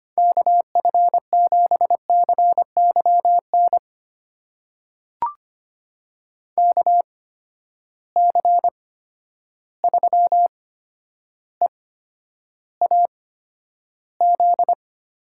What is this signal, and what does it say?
Morse code, heard as a keyed tone, about 25 wpm, spaced as for 11 wpm, 700 Hz. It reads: KF7CYN KC3EAZ
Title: USA Call Signs - 6 Characters - 8x Character Spacing - 25wpm